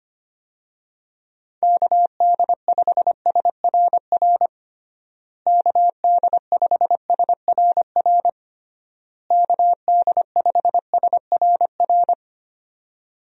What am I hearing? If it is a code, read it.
Morse code, heard as a keyed tone, about 25 wpm, 700 Hz. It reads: KD5SRR KD5SRR KD5SRR
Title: USA Call Signs - 6 Characters - Repeated 3x - 25wpm